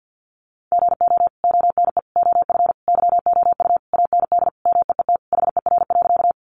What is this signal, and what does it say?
Morse code, heard as a keyed tone, about 50 wpm, 700 Hz. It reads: GO ONE OF YOU AND MEET HER.
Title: Sentences from Top 1100 Words 50wpm